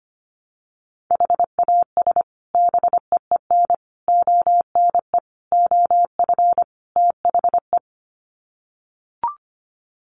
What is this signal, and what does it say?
Morse code, heard as a keyed tone, about 25 wpm, 700 Hz. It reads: HAS BEEN ONE OF THE
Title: Sets of 5 Words Encore 25wpm